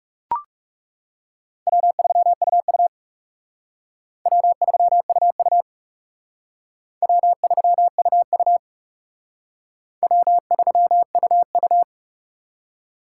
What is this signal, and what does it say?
Morse code, heard as a keyed tone, about 39 wpm, 700 Hz. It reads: W3UU W3UU W3UU W3UU